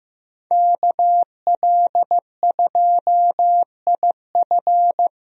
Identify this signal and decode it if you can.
Morse code, heard as a keyed tone, about 15 wpm, 700 Hz. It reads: KL2IF
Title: USA Call Signs - 5 Characters - 15wpm